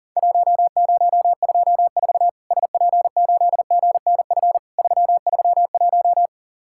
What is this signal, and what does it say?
Morse code, heard as a keyed tone, about 40 wpm, 700 Hz. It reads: 1024 SP8GNF 331
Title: Worked All Europe DX Contest - QTC Exchange - 40wpm